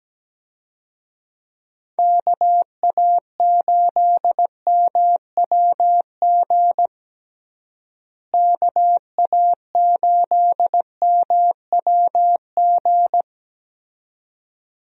Morse code, heard as a keyed tone, about 17 wpm, 700 Hz. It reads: KA8MWG KA8MWG
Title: USA Call Signs - 6 Characters - Repeated 2x - 17wpm